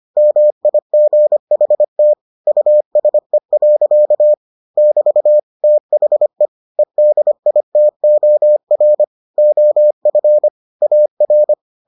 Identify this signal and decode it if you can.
Morse code, heard as a keyed tone, about 25 wpm, 600 Hz. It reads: MIGHT USE. <BT> THE EDITOR OF AR